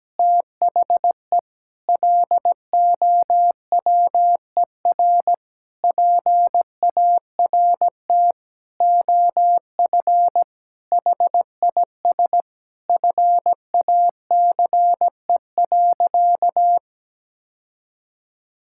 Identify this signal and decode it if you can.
Morse code, heard as a keyed tone, about 17 wpm, 700 Hz. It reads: THE LOWER PART OF HIS FACE.
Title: Sentences from Top 1000 Words - Review Part 5 - 17wpm